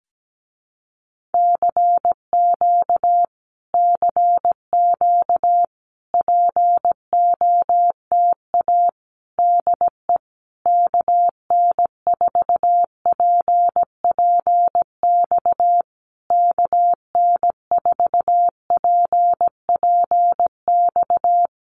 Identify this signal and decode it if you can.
Morse code, heard as a keyed tone, about 17 wpm, 700 Hz. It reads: CQ CQ POTA DE KN4PPX KN4PPX
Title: Common QSO Phrases Encore 17wpm